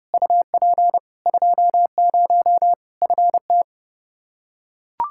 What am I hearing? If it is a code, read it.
Morse code, heard as a keyed tone, about 30 wpm, 700 Hz. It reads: UP 20 FT
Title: K1USN SST QSO Practice 30wpm